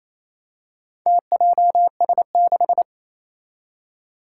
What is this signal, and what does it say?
Morse code, heard as a keyed tone, about 28 wpm, 700 Hz. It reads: TJS6
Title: Four Challenging Letter-Number Combinations 28wpm